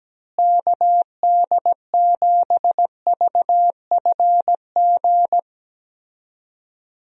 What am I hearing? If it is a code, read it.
Morse code, heard as a keyed tone, about 17 wpm, 700 Hz. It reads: KD7VFG
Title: USA Call Signs - 6 Characters - Repeated 2x - 17wpm